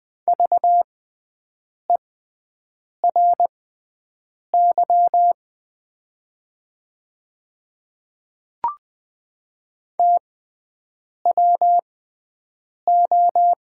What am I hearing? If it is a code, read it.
Morse code, heard as a keyed tone, about 20 wpm, spaced as for 7 wpm, 700 Hz. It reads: VERY TWO
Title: 6x Character Spacing - Top 200 Words - 20wpm